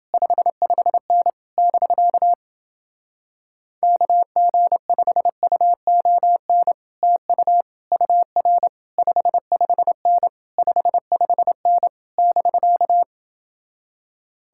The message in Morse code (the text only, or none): 55N <BK> KG5UON TU UR 55N 55N <BK>